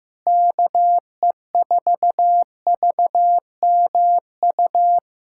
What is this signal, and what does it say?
Morse code, heard as a keyed tone, about 15 wpm, 700 Hz. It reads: KE4VMU
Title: USA Callsign DE Callsign - 6 Characters - 15wpm